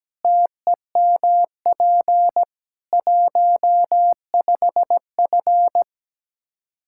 Morse code, heard as a keyed tone, about 17 wpm, 700 Hz. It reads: TEMP 15F
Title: Practice QSO - Exchange 2 - Info - 17wpm